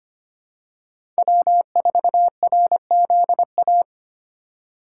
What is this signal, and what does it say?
Morse code, heard as a keyed tone, about 25 wpm, 700 Hz. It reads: W4RZA
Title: USA Call Signs - 3x Character Spacing - 25wpm